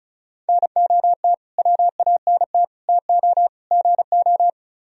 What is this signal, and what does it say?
Morse code, heard as a keyed tone, about 45 wpm, 700 Hz. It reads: NOT WANT TO GO